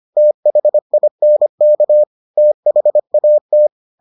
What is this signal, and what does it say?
Morse code, heard as a keyed tone, about 25 wpm, 600 Hz. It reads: THINK THAT